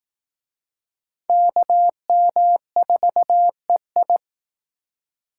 Morse code, heard as a keyed tone, about 18 wpm, 700 Hz. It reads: KM4EI
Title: Speed Racing - USA 2x2 Call Signs 15wpm